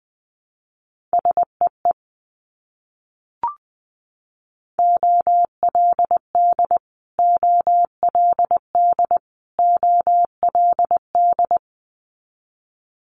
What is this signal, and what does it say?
Morse code, heard as a keyed tone, about 20 wpm, 700 Hz. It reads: SEE OLD OLD OLD
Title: Multi-letter Phonemes Repeated 3x 20wpm